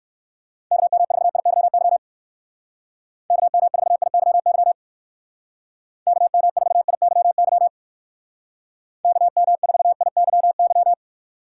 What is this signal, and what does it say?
Morse code, heard as a keyed tone, about 59 wpm, 700 Hz. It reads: KM4IYY KM4IYY KM4IYY KM4IYY